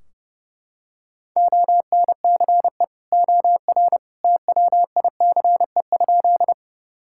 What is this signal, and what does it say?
Morse code, heard as a keyed tone, about 30 wpm, 700 Hz. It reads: ONCE OR TWICE?
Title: Sentences from Top 1600 Words 30wpm